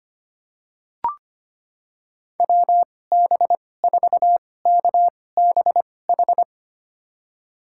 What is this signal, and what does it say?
Morse code, heard as a keyed tone, about 25 wpm, spaced as for 18 wpm, 700 Hz. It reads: WB4KBH
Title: USA Call Signs - 6 Characters - 2x Character Spacing - 25wpm